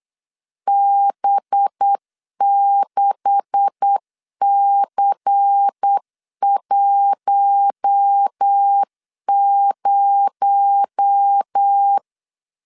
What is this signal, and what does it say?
Morse code, heard as a keyed tone, about 8 wpm, 800 Hz. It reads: B6C10
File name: CW / DET / 1 9 1500C S A 8wpm 5637